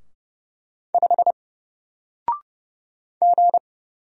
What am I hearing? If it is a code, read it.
Morse code, heard as a keyed tone, about 30 wpm, 700 Hz. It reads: 5 G
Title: Single Character (Letter:Number:Punctuation:Prosign) Alt 30wpm